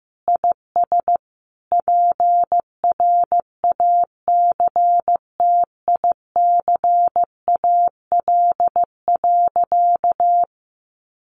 Morse code, heard as a keyed tone, about 15 wpm, 700 Hz. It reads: IS PRACTICAL.
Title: Sentences from Top 1400 Words 15wpm